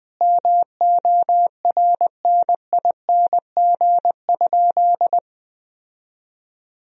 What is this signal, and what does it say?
Morse code, heard as a keyed tone, about 19 wpm, 700 Hz. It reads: MORNING?
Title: Sentences from Top 1000 Words - Review Part 3 - 20wpm